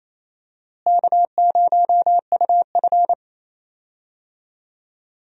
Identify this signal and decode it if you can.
Morse code, heard as a keyed tone, about 27 wpm, 700 Hz. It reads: K0UF
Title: USA Call Signs - 4 Characters - Repeated 3x - 28wpm